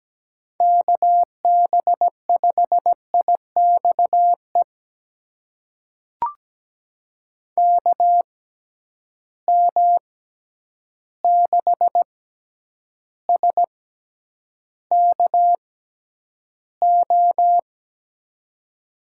Morse code, heard as a keyed tone, about 17 wpm, spaced as for 9 wpm, 700 Hz. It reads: KB5IXE KM6SKO
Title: USA Call Signs - 6 Characters - 6x Character Spacing - 17wpm